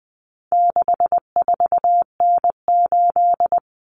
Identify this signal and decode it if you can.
Morse code, heard as a keyed tone, about 20 wpm, 700 Hz. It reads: 64N8
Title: Common Cut Numbers - 4-Digit 20wpm